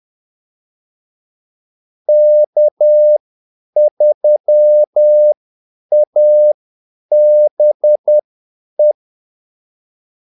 Morse code, heard as a keyed tone, about 10 wpm, spaced as for 8 wpm, 600 Hz. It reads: K3ABE
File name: Callsigns-10-8wpm